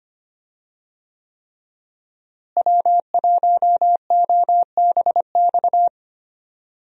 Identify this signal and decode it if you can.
Morse code, heard as a keyed tone, about 25 wpm, 700 Hz. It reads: W1OBX
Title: USA Call Signs - 5 Characters - Repeated 3x - 25wpm